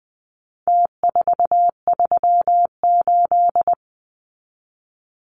Morse code, heard as a keyed tone, about 20 wpm, 700 Hz. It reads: T438
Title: Numbers - Contest Style - 4-Digit 20wpm